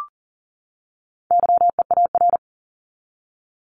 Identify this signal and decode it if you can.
Morse code, heard as a keyed tone, about 40 wpm, 700 Hz. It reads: YEAR
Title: ES:S Suffix - Top 100 Words with Base Words 40wpm